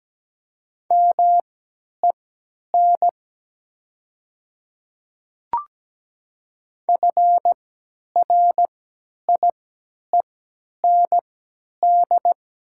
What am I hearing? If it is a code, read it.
Morse code, heard as a keyed tone, about 17 wpm, spaced as for 10 wpm, 700 Hz. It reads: MEN FRIEND
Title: 3x Character Spacing - Top 500 Words - 17wpm